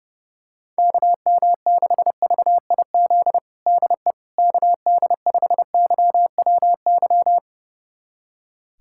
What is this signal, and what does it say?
Morse code, heard as a keyed tone, about 30 wpm, 700 Hz. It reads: KM6VIZ DE KD5YWY
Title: USA Callsign DE Callsign - 6 Characters - 30wpm